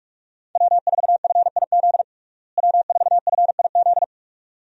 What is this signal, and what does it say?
Morse code, heard as a keyed tone, about 47 wpm, 700 Hz. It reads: W4FIZ W4FIZ
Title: USA Call Signs - 5 Characters - Repeated 2x - 45wpm